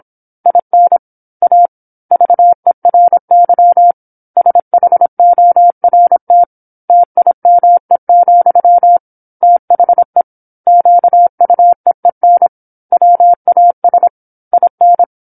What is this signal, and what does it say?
Morse code, heard as a keyed tone, about 27 wpm, 700 Hz. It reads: IN A VERY SHORT TIME, THE QUEEN WAS IN
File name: 08-Chapter-08 26 WPM